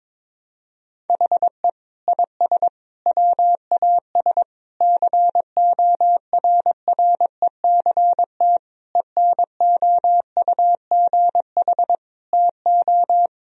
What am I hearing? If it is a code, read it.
Morse code, heard as a keyed tone, about 22 wpm, 700 Hz. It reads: HE IS WAS CORRECT ENOUGH TO
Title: Sentences from Top 1000 Words - Review Part 2 - 22wpm